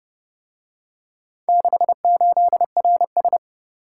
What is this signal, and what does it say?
Morse code, heard as a keyed tone, about 30 wpm, 700 Hz. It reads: 68RS